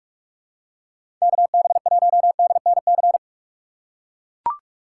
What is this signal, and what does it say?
Morse code, heard as a keyed tone, about 44 wpm, 700 Hz. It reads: KB1DNC
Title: USA Call Signs - 6x Character Spacing - 45wpm